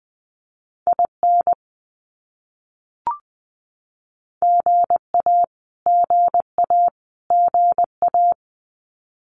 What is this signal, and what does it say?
Morse code, heard as a keyed tone, about 20 wpm, 700 Hz. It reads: IN GA GA GA
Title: US States Repeated 3x 20wpm